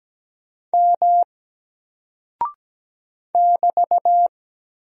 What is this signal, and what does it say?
Morse code, heard as a keyed tone, about 17 wpm, 700 Hz. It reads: M <BT>
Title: Single Character (Letter:Number:Punctuation:Prosign) Alt 17wpm